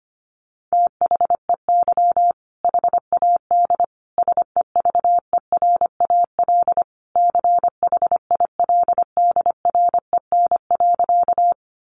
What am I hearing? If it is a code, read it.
Morse code, heard as a keyed tone, about 25 wpm, 700 Hz. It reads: THEY HAD SEVERAL CHILDREN.
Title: Sentences from Top 400 Words 25wpm